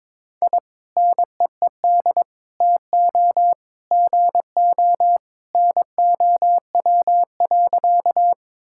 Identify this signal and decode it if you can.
Morse code, heard as a keyed tone, about 22 wpm, 700 Hz. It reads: I NEED TO GO NOW.
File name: practice / 22wpm / Sentences with 2-Word Phrases - 22wpm